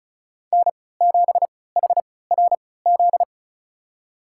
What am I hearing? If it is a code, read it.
Morse code, heard as a keyed tone, about 34 wpm, spaced as for 20 wpm, 700 Hz. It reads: N7HRZ
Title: USA Call Signs - 5 Characters - 3x Character Spacing - 35wpm